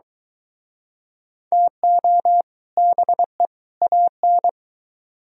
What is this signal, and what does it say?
Morse code, heard as a keyed tone, about 22 wpm, 700 Hz. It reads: TO BE AN